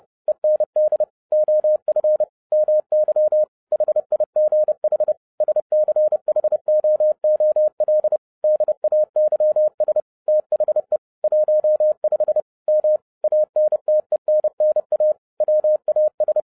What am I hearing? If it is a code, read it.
Morse code, heard as a keyed tone, about 30 wpm, 610 Hz. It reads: END OF MY HIGH SCHOOL DAYS THE 15 M ANTENNA WAS